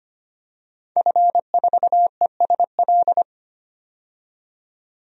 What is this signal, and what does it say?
Morse code, heard as a keyed tone, about 25 wpm, 700 Hz. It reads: F4ESL
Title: DXCC-100 Call Signs - 5 Characters - 25wpm